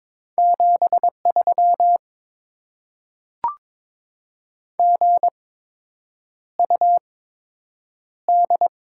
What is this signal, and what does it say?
Morse code, heard as a keyed tone, about 21 wpm, spaced as for 8 wpm, 700 Hz. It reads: 73 GUD
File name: Top 20 QSO Elements - 8x Character Spacing - 22wpm